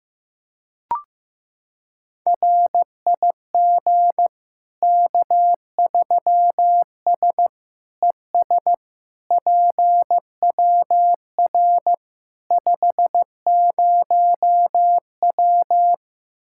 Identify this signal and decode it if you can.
Morse code, heard as a keyed tone, about 16 wpm, 700 Hz. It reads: RIG K3S ES PWR 50W
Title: K1USN SST QSO Practice 15wpm